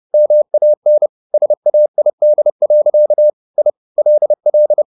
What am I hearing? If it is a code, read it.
Morse code, heard as a keyed tone, about 30 wpm, 600 Hz. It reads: MAN SAID. I LL
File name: the old man and the sea / the old man and the sea 1 30wpm